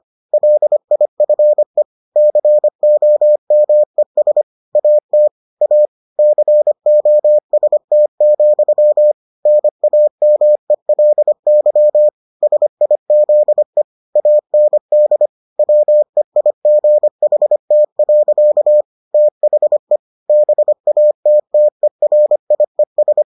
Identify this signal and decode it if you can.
Morse code, heard as a keyed tone, about 25 wpm, 600 Hz. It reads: LIFE COMES AT A COST, NAMELY SIZE AND WEIGHT. THE BATTERIES